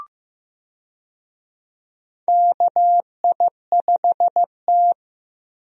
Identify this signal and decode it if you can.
Morse code, heard as a keyed tone, about 15 wpm, 700 Hz. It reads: KI5T
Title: USA Call Signs - 4 Characters - 15wpm